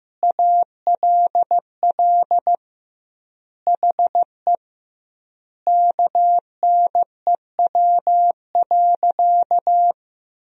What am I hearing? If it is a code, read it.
Morse code, heard as a keyed tone, about 16 wpm, 700 Hz. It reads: ALL HE KNEW.
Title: 2x Word-Spacing - Sentences from Top 400 Words - 15wpm